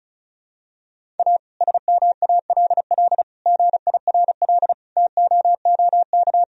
Morse code, heard as a keyed tone, about 34 wpm, 700 Hz. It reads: A SMALL GIRL TOOK
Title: Sentences from Top 1000 Words - Review Part 1 - 35wpm